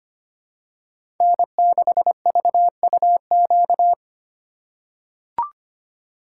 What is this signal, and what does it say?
Morse code, heard as a keyed tone, about 25 wpm, 700 Hz. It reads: N6VUQ
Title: USA Call Signs - 5 Characters - 3x Character Spacing - 25wpm